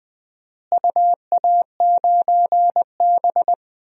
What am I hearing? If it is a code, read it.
Morse code, heard as a keyed tone, about 21 wpm, 700 Hz. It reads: UA9B
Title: DXCC-100 Call Signs - 4 Characters - Repeated 3x - 20wpm